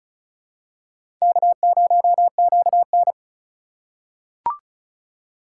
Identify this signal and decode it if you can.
Morse code, heard as a keyed tone, about 36 wpm, 700 Hz. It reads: K0QN
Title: USA Call Signs - 4 Characters - 4x Character Spacing - 35wpm